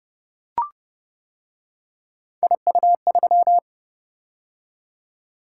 Morse code, heard as a keyed tone, about 28 wpm, 700 Hz. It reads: IU3